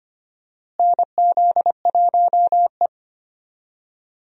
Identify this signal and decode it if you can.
Morse code, heard as a keyed tone, about 25 wpm, 700 Hz. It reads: NZ1E
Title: USA Call Signs - 4 Characters - Repeated 2x - 25wpm